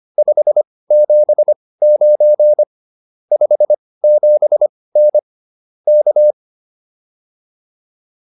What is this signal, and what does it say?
Morse code, heard as a keyed tone, about 25 wpm, spaced as for 18 wpm, 600 Hz. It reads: 579 57N K